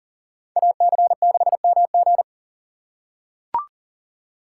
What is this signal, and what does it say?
Morse code, heard as a keyed tone, about 40 wpm, 700 Hz. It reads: AC6MG
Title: USA Call Signs - 5 Characters - 4x Character Spacing - 40wpm